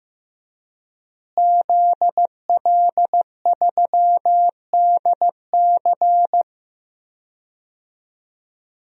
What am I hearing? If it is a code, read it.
Morse code, heard as a keyed tone, about 15 wpm, 700 Hz. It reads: ZL3DC
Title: DXCC-100 Call Signs - 5 Characters - Repeated 3x - 15wpm